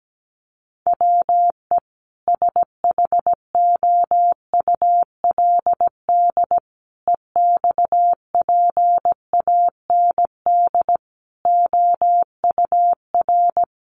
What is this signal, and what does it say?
Morse code, heard as a keyed tone, about 17 wpm, 700 Hz. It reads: WE SHOULD EXPAND OUR
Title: Sentences from Top 1400 Words 17wpm